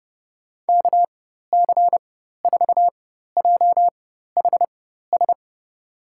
Morse code, heard as a keyed tone, about 31 wpm, spaced as for 14 wpm, 700 Hz. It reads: KC4JHS
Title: USA Call Signs - 6 Characters - 4x Character Spacing - 30wpm